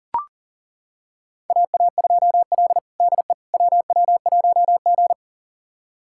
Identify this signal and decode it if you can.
Morse code, heard as a keyed tone, about 40 wpm, 700 Hz. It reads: AA2L DE WW1G
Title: USA Callsign DE Callsign - 4 Characters - 40wpm